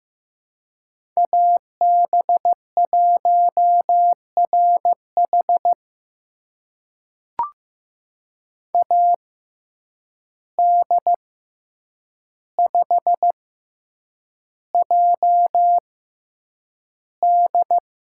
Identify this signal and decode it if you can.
Morse code, heard as a keyed tone, about 15 wpm, spaced as for 8 wpm, 700 Hz. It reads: AB1RH AD5JD
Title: USA Call Signs - 5 Characters - 6x Character Spacing - 15wpm